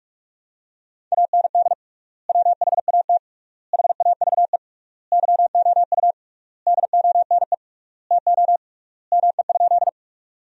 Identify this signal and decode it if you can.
Morse code, heard as a keyed tone, about 46 wpm, 700 Hz. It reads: AND WHAT HAVE YOU DONE TO ME?
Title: 3x Word-Spacing - Sentences from Top 500 Words - 45wpm